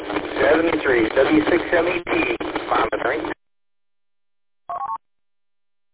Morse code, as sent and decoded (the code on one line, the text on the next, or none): none